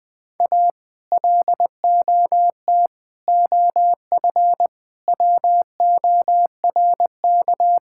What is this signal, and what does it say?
Morse code, heard as a keyed tone, about 20 wpm, 700 Hz. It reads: A LOT OF WORK
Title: Sets of 6 Words - 20wpm